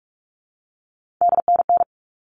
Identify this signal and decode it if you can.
Morse code, heard as a keyed tone, about 44 wpm, 700 Hz. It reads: DNN
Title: All Cut Numbers - 3-Digit 45wpm